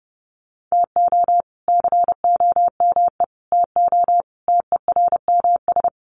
Morse code, heard as a keyed tone, about 28 wpm, 700 Hz. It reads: TO COME TO TERMS
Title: Sets of 5 Words Encore 30wpm